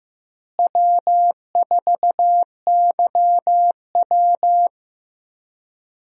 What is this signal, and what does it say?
Morse code, heard as a keyed tone, about 15 wpm, 700 Hz. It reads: W4YW
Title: USA Call Signs - 4 Characters - Repeated 2x - 15wpm